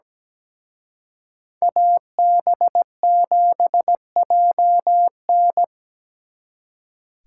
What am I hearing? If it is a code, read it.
Morse code, heard as a keyed tone, about 20 wpm, 700 Hz. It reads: AB7JN
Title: Speed Racing - USA 2x2 Call Signs 17wpm